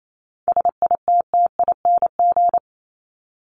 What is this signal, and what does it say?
Morse code, heard as a keyed tone, about 28 wpm, 700 Hz. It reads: SITTING
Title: ING Suffix - Top 100 Words with Base Words 28wpm